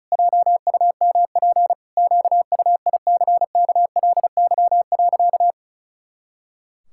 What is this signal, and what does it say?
Morse code, heard as a keyed tone, about 34 wpm, 700 Hz. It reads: JUMP QUICKLY.
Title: Pangram Challenge v2 - 35wpm